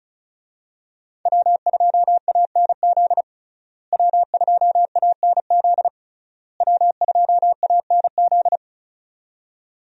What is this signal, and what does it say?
Morse code, heard as a keyed tone, about 36 wpm, 700 Hz. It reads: W2ANZ W2ANZ W2ANZ
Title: USA Call Signs - 5 Characters - Repeated 3x - 35wpm